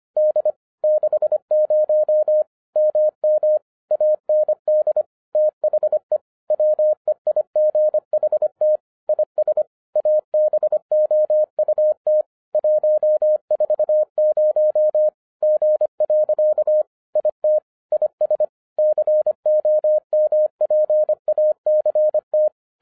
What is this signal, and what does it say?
Morse code, heard as a keyed tone, about 25 wpm, 610 Hz. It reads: D 60 MM AND THE WEIGHT IS ABOUT 140 G. IT IS COMPACT